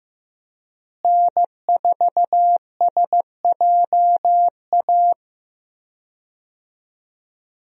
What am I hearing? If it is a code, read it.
Morse code, heard as a keyed tone, about 15 wpm, 700 Hz. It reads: N4SJA
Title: USA Call Signs - 5 Characters - Repeated 2x - 15wpm